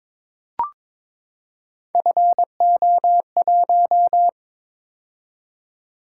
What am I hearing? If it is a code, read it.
Morse code, heard as a keyed tone, about 23 wpm, 700 Hz. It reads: FO1
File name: practice / 22wpm / Three Letter-Number Combinations - 22wpm